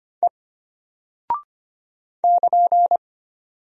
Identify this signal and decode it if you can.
Morse code, heard as a keyed tone, about 25 wpm, 700 Hz. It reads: E <KN>
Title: Single Character (Letter:Number:Punctuation:Prosign) Alt 25wpm